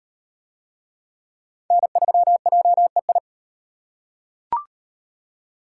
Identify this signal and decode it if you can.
Morse code, heard as a keyed tone, about 38 wpm, 700 Hz. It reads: N3JEI